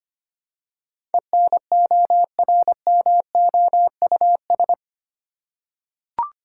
Morse code, heard as a keyed tone, about 25 wpm, 700 Hz. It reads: ENORMOUS
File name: 8 Letter Words - First 3 Letters Given - 25wpm